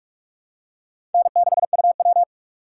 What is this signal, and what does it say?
Morse code, heard as a keyed tone, about 44 wpm, 700 Hz. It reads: N6UW
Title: USA Call Signs - 4 Characters - 6x Character Spacing - 45wpm